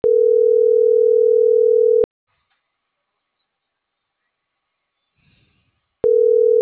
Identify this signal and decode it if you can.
Ringback tone (North America), 440 + 480 Hz.